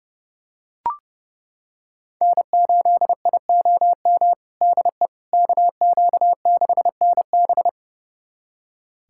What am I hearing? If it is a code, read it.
Morse code, heard as a keyed tone, about 32 wpm, 700 Hz. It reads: N8IOM DE KQ6NB
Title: USA Callsign DE Callsign - 5 Characters - 30wpm